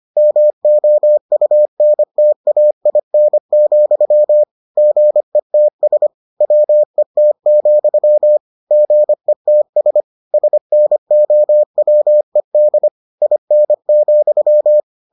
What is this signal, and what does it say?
Morse code, heard as a keyed tone, about 25 wpm, 600 Hz. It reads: MOUNTAIN, GETS WET, GETS SNOWED IN,